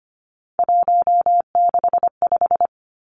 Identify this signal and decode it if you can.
Morse code, heard as a keyed tone, about 25 wpm, 700 Hz. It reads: 165
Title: Three Challenging Letter-Number Combinations 25wpm